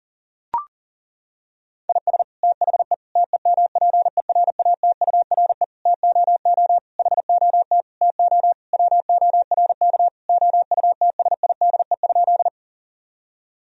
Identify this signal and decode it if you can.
Morse code, heard as a keyed tone, about 37 wpm, 700 Hz. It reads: IS THE TEMPERATURE TOO HOT TO WORK OUTSIDE?
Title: Sentences from Top 1000 Words - Review Part 4 - 40wpm